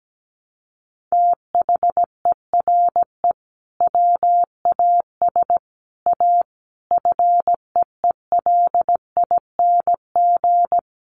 Morse code, heard as a keyed tone, about 17 wpm, 700 Hz. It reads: THERE WAS A FEELING
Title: Sentences from Top 1200 Words 17wpm